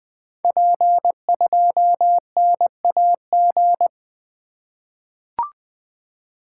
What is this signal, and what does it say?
Morse code, heard as a keyed tone, about 20 wpm, 700 Hz. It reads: P2NAG
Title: Code Groups - 5 Letter:Numbers - 20wpm